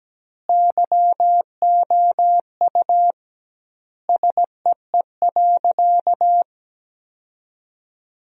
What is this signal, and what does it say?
Morse code, heard as a keyed tone, about 17 wpm, 700 Hz. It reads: YOU SEE.
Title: 2x Word-Spacing - Sentences from Top 100 Words - 17wpm